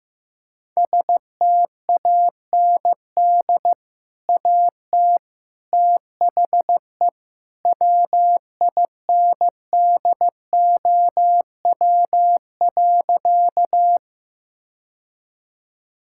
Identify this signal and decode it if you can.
Morse code, heard as a keyed tone, about 15 wpm, 700 Hz. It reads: STAND AT THE WINDOW.
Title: Sentences from Top 1600 Words 15wpm